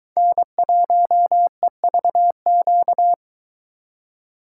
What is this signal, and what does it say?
Morse code, heard as a keyed tone, about 23 wpm, 700 Hz. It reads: N1EVQ